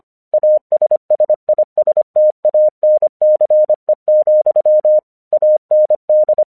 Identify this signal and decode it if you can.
Morse code, heard as a keyed tone, about 25 wpm, 610 Hz. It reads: ASSISTANCE, AND
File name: 1 English 25wpm / AQC05